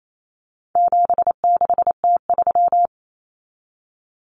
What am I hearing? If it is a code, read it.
Morse code, heard as a keyed tone, about 28 wpm, 700 Hz. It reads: 76T3
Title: Common Cut Numbers - 4-Digit 28wpm